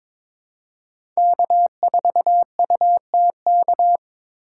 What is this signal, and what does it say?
Morse code, heard as a keyed tone, about 22 wpm, 700 Hz. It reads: K4UTK
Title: USA Call Signs - 5 Characters - 3x Character Spacing - 22wpm